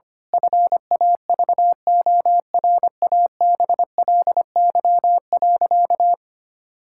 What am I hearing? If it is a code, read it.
Morse code, heard as a keyed tone, about 25 wpm, 700 Hz. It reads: FAVORABLY.
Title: Pangram Challenge v2 - 25wpm